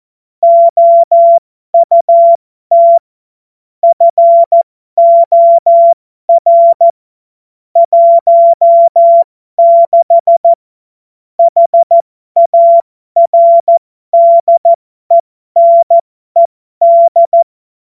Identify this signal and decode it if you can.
Morse code, heard as a keyed tone, about 14 wpm, spaced as for 12 wpm, 680 Hz. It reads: OUT FOR 16 HARDENED
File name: Morsekurs / Witze